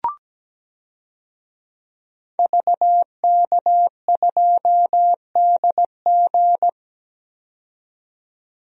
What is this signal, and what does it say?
Morse code, heard as a keyed tone, about 17 wpm, 700 Hz. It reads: VK2DG
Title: DXCC-100 Call Signs - 5 Characters - 17wpm